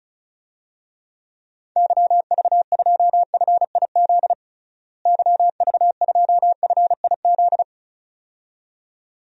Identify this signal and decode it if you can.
Morse code, heard as a keyed tone, about 36 wpm, 700 Hz. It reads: YV2FIZ YV2FIZ
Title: DXCC-100 Call Signs - 6 Characters - Repeated 2x - 35wpm